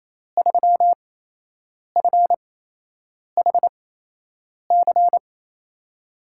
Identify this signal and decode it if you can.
Morse code, heard as a keyed tone, about 28 wpm, spaced as for 8 wpm, 700 Hz. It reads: 3FHC
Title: USA Call Signs - 5 Characters - 8x Character Spacing - 28wpm